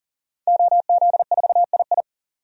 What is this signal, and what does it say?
Morse code, heard as a keyed tone, about 40 wpm, 700 Hz. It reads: OZ4II
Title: DXCC-100 Call Signs - 5 Characters - 40wpm